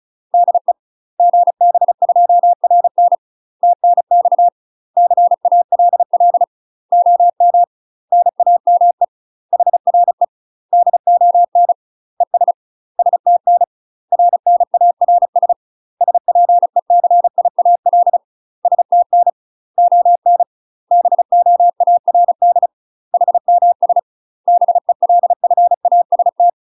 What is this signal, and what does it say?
Morse code, heard as a keyed tone, about 34 wpm, 700 Hz. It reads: DE GB2RN TNX CALL OM NAME HRE DON ES STN RNARS SPECIAL STN ON BOARD HMS BELFAST
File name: UK MASTER QSOs TRAINING / 35WPM / UK MASTER QSOs 0015